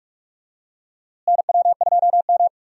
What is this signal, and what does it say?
Morse code, heard as a keyed tone, about 42 wpm, 700 Hz. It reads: NW2M
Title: USA Call Signs - 4 Characters - Repeated 2x - 45wpm